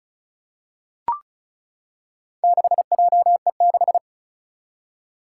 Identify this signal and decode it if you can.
Morse code, heard as a keyed tone, about 34 wpm, 700 Hz. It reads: 6JE6
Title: Four Challenging Letter-Number Combinations 35wpm